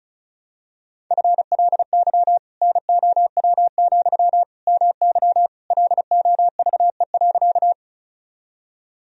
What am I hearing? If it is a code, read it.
Morse code, heard as a keyed tone, about 36 wpm, 700 Hz. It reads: FLY NOW, MY LOVE.
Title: Sentences from Top 1000 Words - Review Part 2 - 35wpm